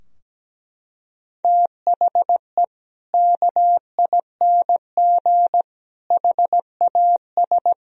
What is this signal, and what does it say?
Morse code, heard as a keyed tone, about 17 wpm, 700 Hz. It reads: THE KING HAS
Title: Sentences from Top 1000 Words - Review Part 2 - 17wpm